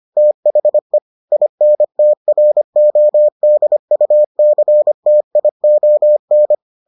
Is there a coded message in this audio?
Yes: Morse code from 0.2 to 6.5 s, heard as a keyed tone, about 25 wpm, 600 Hz. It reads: THE INTRODUCTION